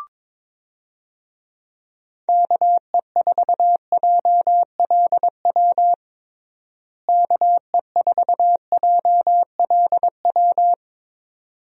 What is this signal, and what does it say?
Morse code, heard as a keyed tone, about 22 wpm, 700 Hz. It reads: KE4JLW KE4JLW
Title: USA Call Signs - 6 Characters - Repeated 2x - 22wpm